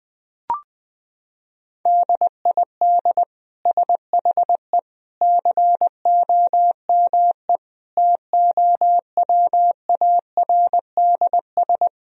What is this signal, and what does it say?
Morse code, heard as a keyed tone, about 20 wpm, 700 Hz. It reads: DID SHE COME TOWARDS
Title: Sentences from Top 1000 Words - Review Part 2 - 20wpm